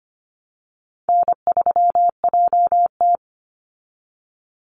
Morse code, heard as a keyed tone, about 25 wpm, 700 Hz. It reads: N3JT
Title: Call Signs Repeated 2x - 25wpm